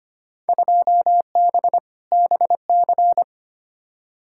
Morse code, heard as a keyed tone, about 25 wpm, 700 Hz. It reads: 2B BC
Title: Field Day Exchanges Encore - 25wpm